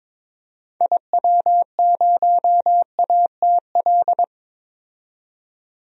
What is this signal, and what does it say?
Morse code, heard as a keyed tone, about 22 wpm, 700 Hz. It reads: IW0ATL
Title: DXCC-100 Call Signs - 6 Characters - 22wpm